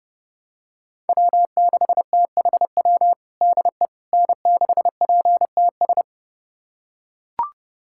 Morse code, heard as a keyed tone, about 29 wpm, 700 Hz. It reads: W6THW DE N6PTS
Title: USA Callsign DE Callsign - 5 Characters - 30wpm